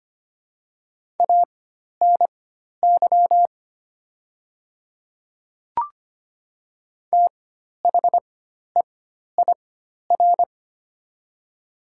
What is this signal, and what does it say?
Morse code, heard as a keyed tone, about 25 wpm, spaced as for 12 wpm, 700 Hz. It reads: ANY THEIR
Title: 4x Character Spacing - Top 300 Words - 25wpm